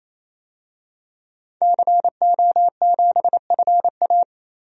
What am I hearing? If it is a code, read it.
Morse code, heard as a keyed tone, about 28 wpm, 700 Hz. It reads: CO7FA
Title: DXCC-100 Call Signs - 5 Characters - 28wpm